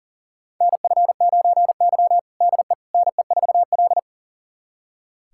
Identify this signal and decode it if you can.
Morse code, heard as a keyed tone, about 40 wpm, 700 Hz. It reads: NF9Y DE NE4L
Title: USA Callsign DE Callsign - 4 Characters - 40wpm